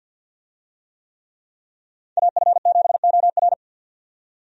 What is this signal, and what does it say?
Morse code, heard as a keyed tone, about 51 wpm, 700 Hz. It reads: AF7OR
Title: USA Call Signs - 5 Characters - 50wpm